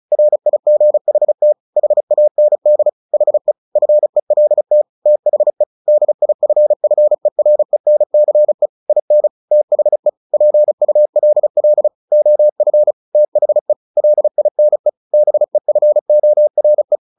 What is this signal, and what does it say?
Morse code, heard as a keyed tone, about 34 wpm, 600 Hz. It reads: RIGHT HAND HE FELT THE DIFFERENCE IN THE PULL OF THE LINE BEFORE
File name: the old man and the sea / the old man and the sea 1 35wpm